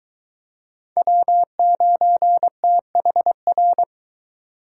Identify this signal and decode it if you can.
Morse code, heard as a keyed tone, about 23 wpm, 700 Hz. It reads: W9THR